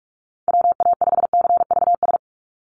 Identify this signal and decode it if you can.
Morse code, heard as a keyed tone, about 46 wpm, 700 Hz. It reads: WA5CVS
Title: USA Call Signs 45wpm